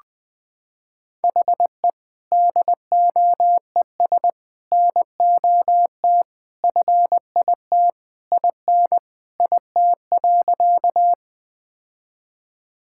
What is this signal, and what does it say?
Morse code, heard as a keyed tone, about 20 wpm, 700 Hz. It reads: HE DOES NOT FIT IN IT.
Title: Sentences from Top 1000 Words - Review Part 5 - 20wpm